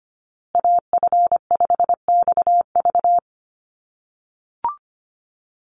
Morse code, heard as a keyed tone, about 25 wpm, 700 Hz. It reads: AF5XV
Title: USA Call Signs Repeated 2x 25wpm